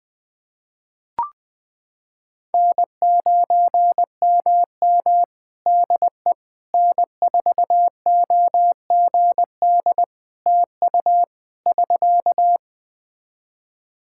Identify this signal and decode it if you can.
Morse code, heard as a keyed tone, about 20 wpm, 700 Hz. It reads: N9MM DE N4OGD TU <SK>
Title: K1USN SST QSO Practice 20wpm